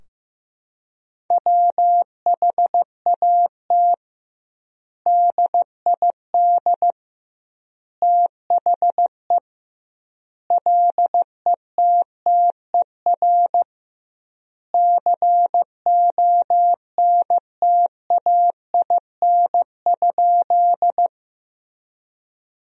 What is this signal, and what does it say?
Morse code, heard as a keyed tone, about 15 wpm, 700 Hz. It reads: WHAT DID THE LETTER CONTAIN?
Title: 2x Word-Spacing - Sentences from Top 500 Words - 15wpm